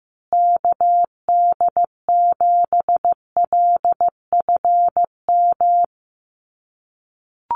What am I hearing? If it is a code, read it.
Morse code, heard as a keyed tone, about 15 wpm, 700 Hz. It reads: KD7LFM
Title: USA Call Signs Repeated 2x 15wpm